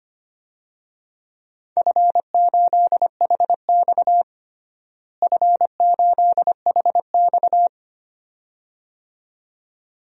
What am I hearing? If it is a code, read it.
Morse code, heard as a keyed tone, about 25 wpm, 700 Hz. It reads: F8HX F8HX
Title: DXCC-100 Call Signs - 4 Characters - Repeated 2x - 25wpm